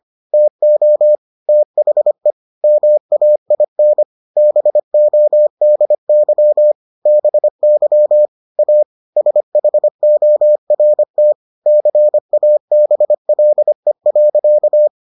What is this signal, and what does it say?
Morse code, heard as a keyed tone, about 25 wpm, 600 Hz. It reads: TO THE MAIN BODY BY A SHORT CABLE.